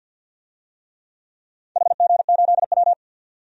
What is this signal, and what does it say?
Morse code, heard as a keyed tone, about 47 wpm, 700 Hz. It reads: HG8W
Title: DXCC-100 Call Signs - 4 Characters - 50wpm